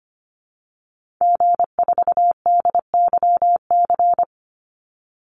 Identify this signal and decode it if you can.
Morse code, heard as a keyed tone, about 25 wpm, 700 Hz. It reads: G4DYC
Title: Call Signs Repeated 3x - 25wpm